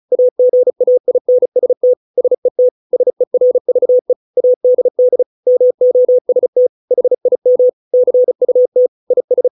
Morse code, heard as a keyed tone, about 34 wpm, 500 Hz. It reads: AGAINST SET SERVE ADD MOST HIM CUT IS